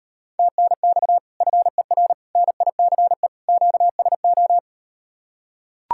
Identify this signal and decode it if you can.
Morse code, heard as a keyed tone, about 39 wpm, 700 Hz. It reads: TNX FER NICE QSO